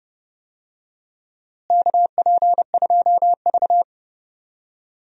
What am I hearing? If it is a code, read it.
Morse code, heard as a keyed tone, about 30 wpm, 700 Hz. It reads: KP2V
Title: DXCC-100 Call Signs - 4 Characters - Repeated 3x - 30wpm